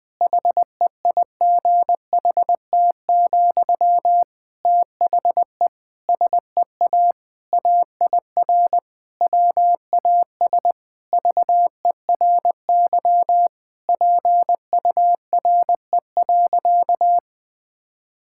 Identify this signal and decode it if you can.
Morse code, heard as a keyed tone, about 20 wpm, 700 Hz. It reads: HEIGHT, THE SEA AIR WAS VERY PURE.
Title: Sentences from Top 1600 Words 20wpm